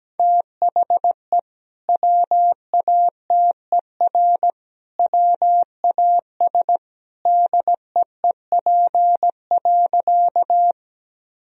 Morse code, heard as a keyed tone, about 17 wpm, 700 Hz. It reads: THE WATER WAS DEEP.
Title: Sentences from Top 1000 Words - Review Part 3 - 17wpm